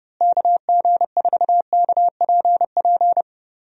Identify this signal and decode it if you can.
Morse code, heard as a keyed tone, about 32 wpm, 700 Hz. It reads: KG4KPP
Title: USA Call Signs - 6 Characters - Repeated 3x - 30wpm